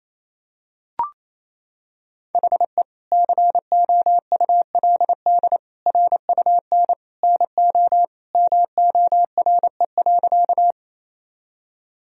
Morse code, heard as a keyed tone, about 28 wpm, 700 Hz. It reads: HE COULD RUN NO MORE.
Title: Sentences from Top 1000 Words - Review Part 1 - 28wpm